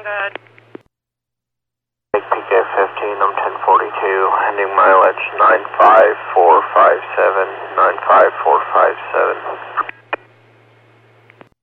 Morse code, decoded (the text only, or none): none